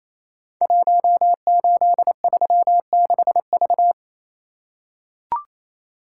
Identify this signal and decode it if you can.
Morse code, heard as a keyed tone, about 29 wpm, 700 Hz. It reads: 1836V
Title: Code Groups - 5 Letter:Numbers - 28wpm